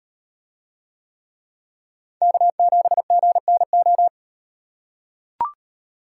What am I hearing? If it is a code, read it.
Morse code, heard as a keyed tone, about 38 wpm, 700 Hz. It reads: K7GNO